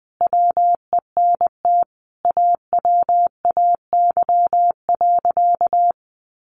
Morse code, heard as a keyed tone, about 20 wpm, 700 Hz. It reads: WENT AWAY.
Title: Sentences from Top 600 Words 20wpm